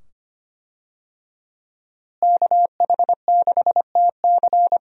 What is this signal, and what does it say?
Morse code, heard as a keyed tone, about 25 wpm, 700 Hz. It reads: KH6TC
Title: DXCC-100 Call Signs - 5 Characters - Repeated 2x - 25wpm